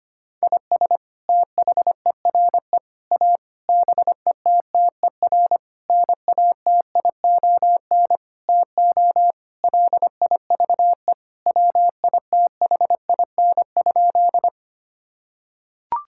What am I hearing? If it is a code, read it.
Morse code, heard as a keyed tone, about 25 wpm, 700 Hz. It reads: IS THERE A BETTER NATION TO LIVE WITHIN?
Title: Sentences from Top 1000 Words - Review Part 6 - 25wpm